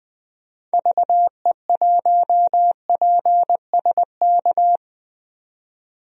Morse code, heard as a keyed tone, about 21 wpm, 700 Hz. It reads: VE1PSK
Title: DXCC-100 Call Signs - 6 Characters - 20wpm